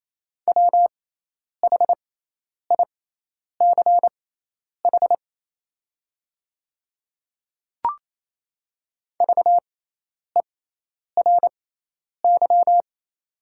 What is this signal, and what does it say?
Morse code, heard as a keyed tone, about 28 wpm, spaced as for 10 wpm, 700 Hz. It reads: WHICH VERY
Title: 6x Character Spacing - Top 200 Words - 28wpm